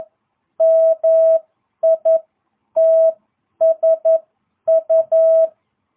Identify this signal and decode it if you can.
Morse code, heard as a keyed tone, about 11 wpm, spaced as for 9 wpm, 650 Hz. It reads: MITSU